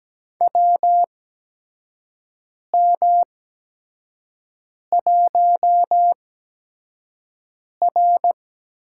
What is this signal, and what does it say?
Morse code, heard as a keyed tone, about 17 wpm, spaced as for 5 wpm, 700 Hz. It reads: WM1R
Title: USA Call Signs - 4 Characters - 8x Character Spacing - 17wpm